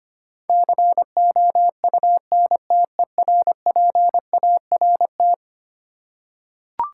Morse code, heard as a keyed tone, about 25 wpm, 700 Hz. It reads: COUNTERPART
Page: morse-code-ninja-practice-25wpm